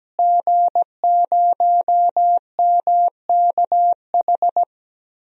Morse code, heard as a keyed tone, about 17 wpm, 700 Hz. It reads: G0MKH
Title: DXCC-100 Call Signs - 5 Characters - Repeated 2x - 17wpm